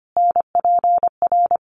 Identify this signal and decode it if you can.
Morse code, heard as a keyed tone, about 25 wpm, 700 Hz. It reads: NPR
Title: Sets of 5 Words Encore 25wpm